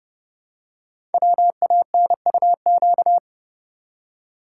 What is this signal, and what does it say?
Morse code, heard as a keyed tone, about 29 wpm, 700 Hz. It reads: WANUQ
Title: Code Groups - 5 Letter:Numbers - 30wpm